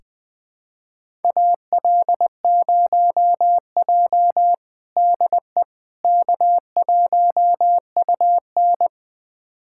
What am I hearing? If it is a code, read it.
Morse code, heard as a keyed tone, about 21 wpm, 700 Hz. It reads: AL0J DE K1UN
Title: USA Callsign DE Callsign - 4 Characters - 20wpm